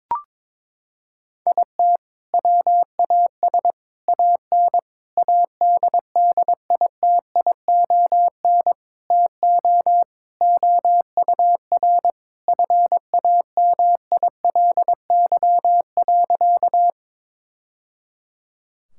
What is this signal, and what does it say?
Morse code, heard as a keyed tone, about 21 wpm, 700 Hz. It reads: IT WAS AN ADDITION TO OUR FAMILY.
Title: Sentences from Top 1000 Words - Review Part 6 - 22wpm